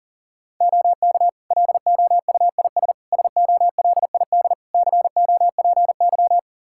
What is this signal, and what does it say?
Morse code, heard as a keyed tone, about 40 wpm, 700 Hz. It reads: OK LOUIS SOLID COPY
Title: Practice QSO - Exchange 3 - More Info - 40wpm